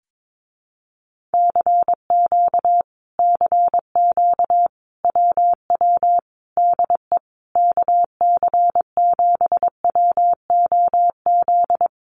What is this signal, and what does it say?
Morse code, heard as a keyed tone, about 22 wpm, 700 Hz. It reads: CQ CQ WW DE KC7WOZ
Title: Common QSO Phrases Encore 22wpm